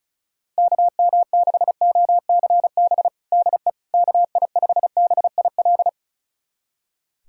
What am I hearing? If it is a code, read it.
Morse code, heard as a keyed tone, about 36 wpm, 700 Hz. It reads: KM6OCB DE KI5BIL
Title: USA Callsign DE Callsign - 6 Characters - 35wpm